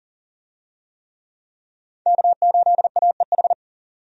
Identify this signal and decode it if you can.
Morse code, heard as a keyed tone, about 40 wpm, 700 Hz. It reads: K8AEH
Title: USA Call Signs - 5 Characters - 40wpm